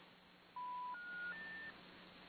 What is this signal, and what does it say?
Special information tone.